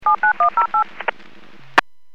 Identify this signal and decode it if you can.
DTMF, heard as telephone touch tones, tones of about 110 ms. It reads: 79108
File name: WPS / 00379 13-44 WPS 451.25 WPS